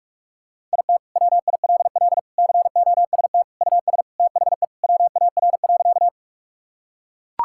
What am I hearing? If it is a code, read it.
Morse code, heard as a keyed tone, about 43 wpm, 700 Hz. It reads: IT WILL COST US THE WAR.
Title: Sentences from Top 1000 Words - Review Part 5 - 45wpm